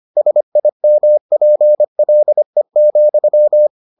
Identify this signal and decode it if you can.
Morse code, heard as a keyed tone, about 25 wpm, 600 Hz. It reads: SIMPLE,